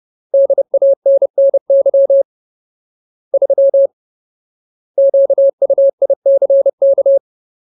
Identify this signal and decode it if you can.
Morse code, heard as a keyed tone, about 30 wpm, 550 Hz. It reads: DANNY 3 QUICK
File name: Ultimate Word List / 30WPM Ultimate